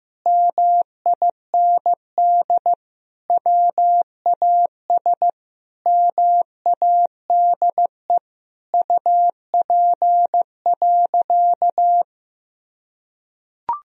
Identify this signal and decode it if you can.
Morse code, heard as a keyed tone, about 16 wpm, 700 Hz. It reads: MIND WAS MADE UP.
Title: Sentences from Top 1000 Words - Review Part 2 - 15wpm